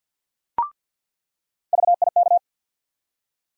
Morse code, heard as a keyed tone, about 47 wpm, 700 Hz. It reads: VIK
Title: USA Call Sign Postfixes - 3 Letters 50wpm